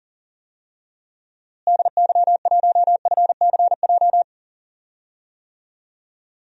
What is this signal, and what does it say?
Morse code, heard as a keyed tone, about 37 wpm, 700 Hz. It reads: DY1FCJ
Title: DXCC-100 Call Signs - 6 Characters - Repeated 3x - 40wpm